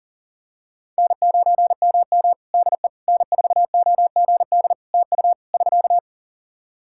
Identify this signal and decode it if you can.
Morse code, heard as a keyed tone, about 40 wpm, 700 Hz. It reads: N9MM DE N4OGD TU <SK>
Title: K1USN SST QSO Practice 40wpm